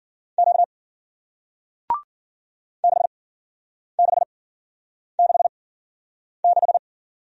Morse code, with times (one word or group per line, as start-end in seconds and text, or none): 0.4-0.6 Q
2.8-6.8 6666